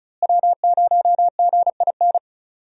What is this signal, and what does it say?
Morse code, heard as a keyed tone, about 34 wpm, 700 Hz. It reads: W0GIN
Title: USA Call Signs - 5 Characters - Repeated 2x - 35wpm